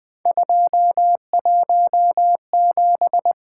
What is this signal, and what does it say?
Morse code, heard as a keyed tone, about 20 wpm, 700 Hz. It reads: 217
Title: Three Letter-Number Combinations - 20wpm